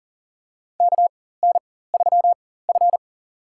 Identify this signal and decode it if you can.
Morse code, heard as a keyed tone, about 40 wpm, spaced as for 19 wpm, 700 Hz. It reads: KN3F
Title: USA Call Signs - 4 Characters - 4x Character Spacing - 40wpm